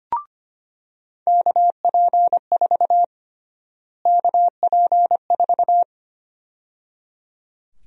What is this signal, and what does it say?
Morse code, heard as a keyed tone, about 25 wpm, 700 Hz. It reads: KP4 KP4
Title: USA Call Sign Prefixes - Repeated 2x - 25wpm